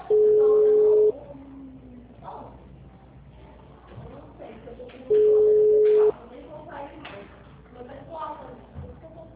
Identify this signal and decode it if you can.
Ringback tone (Europe, 425 Hz), with speech.